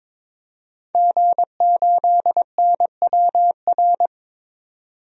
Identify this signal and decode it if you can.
Morse code, heard as a keyed tone, about 22 wpm, 700 Hz. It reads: G8NWR
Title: DXCC-100 Call Signs - 5 Characters - 22wpm